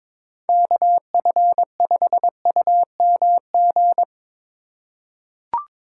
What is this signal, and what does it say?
Morse code, heard as a keyed tone, about 21 wpm, 700 Hz. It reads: KF5UMG
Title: USA Call Signs - 6 Characters - 2x Character Spacing - 22wpm